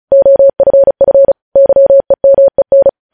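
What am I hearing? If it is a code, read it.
Morse code, heard as a keyed tone, about 36 wpm, 550 Hz. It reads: OFF YEMEN
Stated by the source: Morse news headlines practice at 35.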